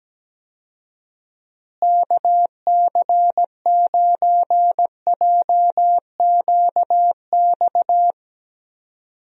Morse code, heard as a keyed tone, about 17 wpm, 700 Hz. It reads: KC9JQX
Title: USA Call Signs - 6 Characters - Repeated 3x - 17wpm